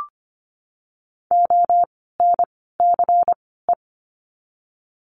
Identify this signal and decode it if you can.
Morse code, heard as a keyed tone, about 25 wpm, spaced as for 16 wpm, 700 Hz. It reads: ONCE
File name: course / 25:10wpm / 094 Lesson 16f - Letter c - Words 25wpm